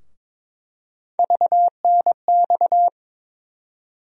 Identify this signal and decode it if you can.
Morse code, heard as a keyed tone, about 21 wpm, 700 Hz. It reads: VNX
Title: USA Call Sign Postfixes - 3 Letters 22wpm